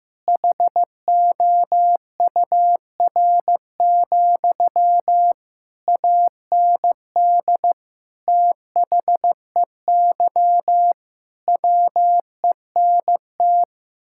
Morse code, heard as a keyed tone, about 15 wpm, 700 Hz. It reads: HOUR, AND THEY WENT
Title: Sentences from Top 1600 Words 15wpm